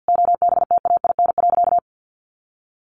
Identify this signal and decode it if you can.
Morse code, heard as a keyed tone, about 48 wpm, 700 Hz. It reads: OBTAIN.
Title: Sentences from Top 1300 Words 50wpm